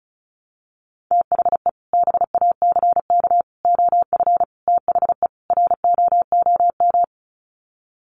Morse code, heard as a keyed tone, about 34 wpm, 700 Hz. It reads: THE BACK OF THE ROOM
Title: Sets of 5 Words Encore 35wpm